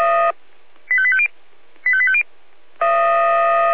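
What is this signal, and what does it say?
Five-tone selective call (ZVEI), heard as quick runs of five tones, tones of about 70 ms. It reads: 85850 85850
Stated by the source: ZVEI five-tone selective call 85880 received on an SDR.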